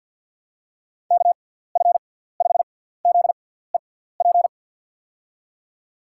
Morse code, heard as a keyed tone, about 46 wpm, spaced as for 17 wpm, 700 Hz. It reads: KF5ZEP
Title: USA Call Signs - 6x Character Spacing - 50wpm